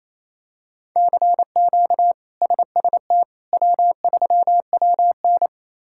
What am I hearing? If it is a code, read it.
Morse code, heard as a keyed tone, about 28 wpm, 700 Hz. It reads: CQ SST W3WN
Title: K1USN SST QSO Practice - 28wpm